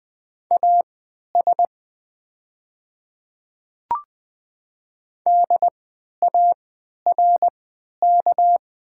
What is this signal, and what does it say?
Morse code, heard as a keyed tone, about 20 wpm, spaced as for 11 wpm, 700 Hz. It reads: AS DARK
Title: 3x Character Spacing - Top 500 Words - 20wpm